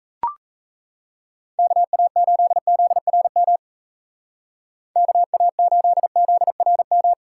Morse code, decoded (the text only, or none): KA8ZRM KA8ZRM